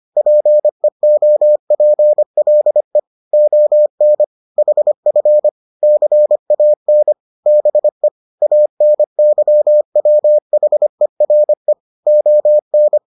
Morse code, heard as a keyed tone, about 25 wpm, 600 Hz. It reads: PEOPLE ON HF CAN BE ANYWHERE ON